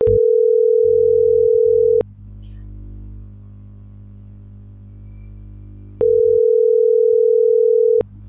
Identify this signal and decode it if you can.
Ringback tone (North America), 440 + 480 Hz.